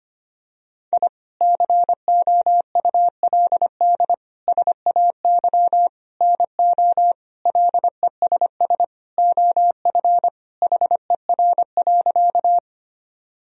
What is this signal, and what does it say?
Morse code, heard as a keyed tone, about 25 wpm, 700 Hz. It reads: I COULD SAY NO LESS OF HER.
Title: Sentences from Top 1000 Words - Review Part 2 - 25wpm